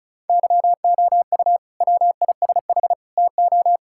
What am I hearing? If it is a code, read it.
Morse code, heard as a keyed tone, about 36 wpm, 690 Hz. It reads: YOU WISH TO
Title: Sentences from Top 1600 Words 35wpm